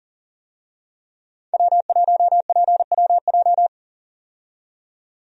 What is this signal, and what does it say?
Morse code, heard as a keyed tone, about 40 wpm, 700 Hz. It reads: W1PWJ
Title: USA Call Signs - 5 Characters - 40wpm